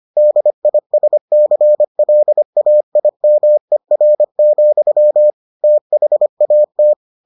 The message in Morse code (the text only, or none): DISCLAIMER, THAT